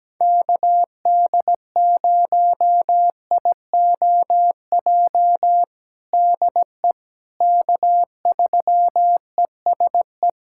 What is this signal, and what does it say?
Morse code, heard as a keyed tone, about 17 wpm, 700 Hz. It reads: KD0IOJ DE K3ESE
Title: USA Callsign DE Callsign 17wpm